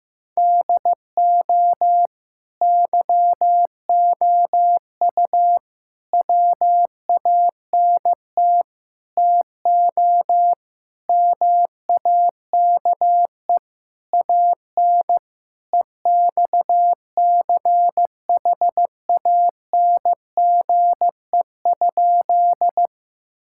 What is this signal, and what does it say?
Morse code, heard as a keyed tone, about 15 wpm, 700 Hz. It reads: DO YOU WANT TO MAKE AN EXCHANGE?
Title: Sentences from Top 1000 Words - Review Part 6 - 15wpm